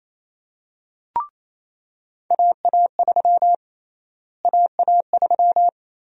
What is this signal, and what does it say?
Morse code, heard as a keyed tone, about 28 wpm, 700 Hz. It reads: AA3 AA3
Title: USA Call Sign Prefixes - Repeated 2x - 28wpm